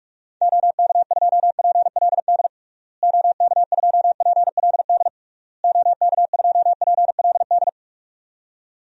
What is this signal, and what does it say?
Morse code, heard as a keyed tone, about 41 wpm, 700 Hz. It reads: OK2PLD OK2PLD OK2PLD
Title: DXCC-100 Call Signs - 6 Characters - Repeated 3x - 45wpm